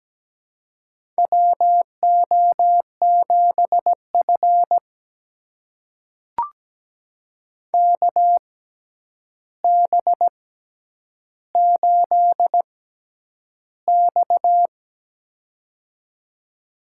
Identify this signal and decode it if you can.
Morse code, heard as a keyed tone, about 17 wpm, spaced as for 9 wpm, 700 Hz. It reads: WO7F KB8X
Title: USA Call Signs - 4 Characters - 6x Character Spacing - 17wpm